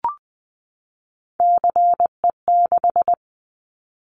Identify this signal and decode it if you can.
Morse code, heard as a keyed tone, about 20 wpm, 700 Hz. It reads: CE6
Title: Callsign Prefixes for Top 100 DXCC - 20wpm